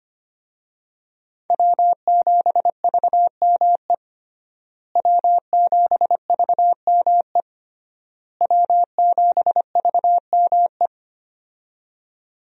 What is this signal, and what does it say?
Morse code, heard as a keyed tone, about 25 wpm, 700 Hz. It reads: W7VME W7VME W7VME
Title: USA Call Signs - 5 Characters - Repeated 3x - 25wpm